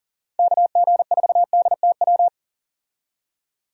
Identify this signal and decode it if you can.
Morse code, heard as a keyed tone, about 40 wpm, 700 Hz. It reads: KG4DTW
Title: USA Call Signs - 6 Characters - 40wpm